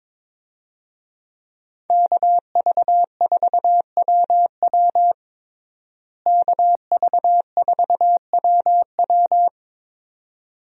Morse code, heard as a keyed tone, about 22 wpm, 700 Hz. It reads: KV4WW KV4WW
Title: USA Call Signs - 5 Characters - Repeated 2x - 22wpm